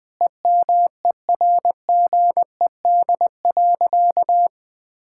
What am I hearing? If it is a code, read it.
Morse code, heard as a keyed tone, about 20 wpm, 700 Hz. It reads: EMERGED.
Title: Sentences with 5-Word Phrases - 20wpm